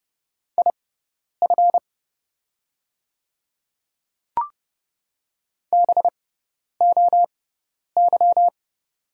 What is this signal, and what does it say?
Morse code, heard as a keyed tone, about 30 wpm, spaced as for 10 wpm, 690 Hz. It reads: IF BOY
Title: 6x Character Spacing - Top 500 Words - 30wpm